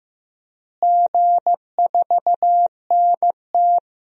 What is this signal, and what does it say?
Morse code, heard as a keyed tone, about 15 wpm, 700 Hz. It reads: G4NT
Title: DXCC-100 Call Signs - 4 Characters - Repeated 2x - 15wpm